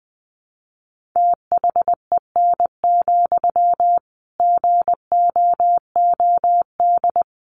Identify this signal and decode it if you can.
Morse code, heard as a keyed tone, about 20 wpm, 700 Hz. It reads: THEN, GOOD
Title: Sentences from Top 300 Words 20wpm